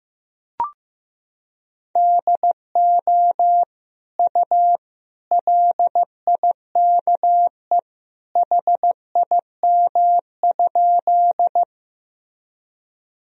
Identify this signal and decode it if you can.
Morse code, heard as a keyed tone, about 16 wpm, 700 Hz. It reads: DO U LIKE HIM?
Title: Sentences from Top 100 Words and Abbreviations 15wpm